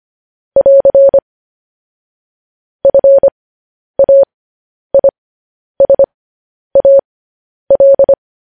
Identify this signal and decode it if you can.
Morse code, heard as a keyed tone, about 25 wpm, spaced as for 10 wpm, 550 Hz. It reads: <AR> FAISAL
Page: news headlines in morse code at 10 wpm